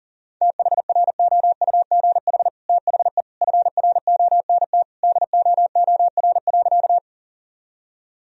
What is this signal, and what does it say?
Morse code, heard as a keyed tone, about 40 wpm, 700 Hz. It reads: THROUGH THE FRONT DOOR.
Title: Sentences from Top 1000 Words - Review Part 2 - 40wpm